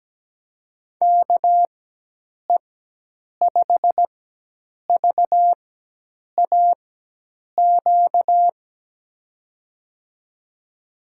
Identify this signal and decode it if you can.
Morse code, heard as a keyed tone, about 17 wpm, spaced as for 8 wpm, 700 Hz. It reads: KE5VAQ
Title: USA Call Signs - 6 Characters - 4x Character Spacing - 17wpm